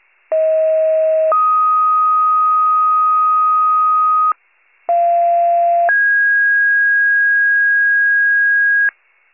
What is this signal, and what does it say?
Two-tone page, heard as a one-second tone, then a three-second tone, about 640 then 1250 Hz, about 690 then 1650 Hz, with noise. It is heard from 0.3 to 8.9 s.